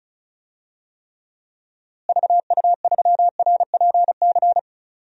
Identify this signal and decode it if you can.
Morse code, heard as a keyed tone, about 36 wpm, 700 Hz. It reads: VU3RPC
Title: DXCC-100 Call Signs - 6 Characters - Repeated 2x - 35wpm